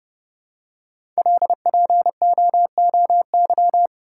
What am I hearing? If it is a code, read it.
Morse code, heard as a keyed tone, about 30 wpm, 700 Hz. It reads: LPOOY
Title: Code Groups - 5 Letters - 30wpm